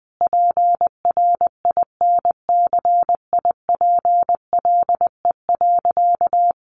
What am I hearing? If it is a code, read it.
Morse code, heard as a keyed tone, about 20 wpm, 700 Hz. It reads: PRINCIPLE.
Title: Sentences from Top 1400 Words 20wpm